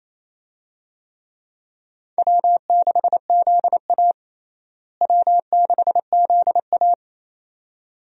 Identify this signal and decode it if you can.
Morse code, heard as a keyed tone, about 28 wpm, 700 Hz. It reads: W6ZA W6ZA
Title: USA Call Signs - 4 Characters - Repeated 2x - 28wpm